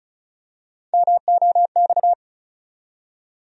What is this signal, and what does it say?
Morse code, heard as a keyed tone, about 36 wpm, 700 Hz. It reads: MOX
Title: USA Call Sign Postfixes - 3 Letters 35wpm